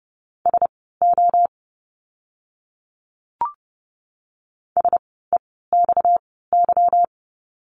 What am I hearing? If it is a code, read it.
Morse code, heard as a keyed tone, about 30 wpm, spaced as for 17 wpm, 700 Hz. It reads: SO SEXY
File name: course / 30:10wpm / 230 Lesson 37f - Letter x - Words 30wpm